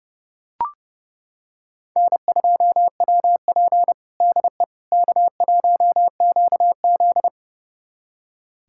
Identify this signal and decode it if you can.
Morse code, heard as a keyed tone, about 30 wpm, 700 Hz. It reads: N2WP DE K1QZ
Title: USA Callsign DE Callsign - 4 Characters - 30wpm